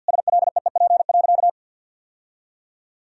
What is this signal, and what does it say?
Morse code, heard as a keyed tone, about 50 wpm, 700 Hz. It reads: SLEEP.
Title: Sentences from Top 1000 Words - Review Part 2 - 50wpm